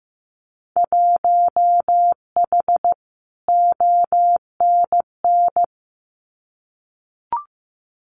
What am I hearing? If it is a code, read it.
Morse code, heard as a keyed tone, about 15 wpm, 700 Hz. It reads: 1H ONN
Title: Winter Field Day Exchanges 15wpm